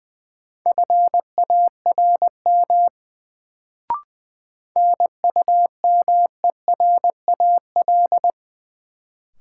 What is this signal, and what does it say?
Morse code, heard as a keyed tone, about 20 wpm, 700 Hz. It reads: FARM NUMERAL
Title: Sentences from Top 1000 Words - Review Part 2 - New Word Drill - 20wpm